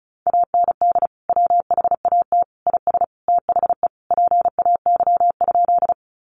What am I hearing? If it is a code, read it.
Morse code, heard as a keyed tone, about 34 wpm, 700 Hz. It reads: AND WHAT IS THE PAY?
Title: Sentences from Top 700 Words 35wpm